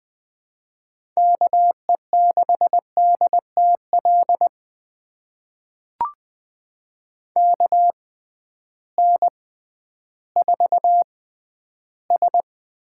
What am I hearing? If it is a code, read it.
Morse code, heard as a keyed tone, about 20 wpm, spaced as for 12 wpm, 700 Hz. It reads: KE6DTL KN4S
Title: USA Call Signs - 6x Character Spacing - 20wpm